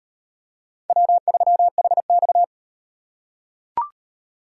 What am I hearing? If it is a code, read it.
Morse code, heard as a keyed tone, about 35 wpm, 700 Hz. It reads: W3HX